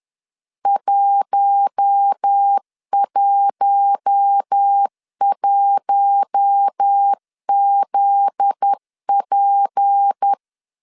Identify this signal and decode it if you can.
Morse code, heard as a keyed tone, about 10 wpm, 800 Hz. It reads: 111ZP
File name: CW / DET / 1 9 1500C S B 10wpm 4439